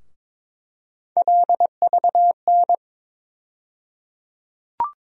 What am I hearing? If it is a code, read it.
Morse code, heard as a keyed tone, about 21 wpm, 700 Hz. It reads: LVN
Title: USA Call Sign Postfixes - 3 Letters 22wpm